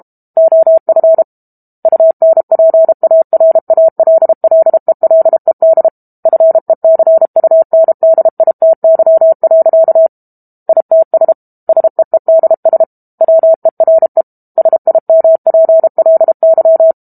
Morse code, heard as a keyed tone, about 32 wpm, 650 Hz. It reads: OF UNPARALLELED FECUNDITY. ITS SEEDS WERE SIMPLY
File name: Book Two / 06-Chapter 6 - 32 WPM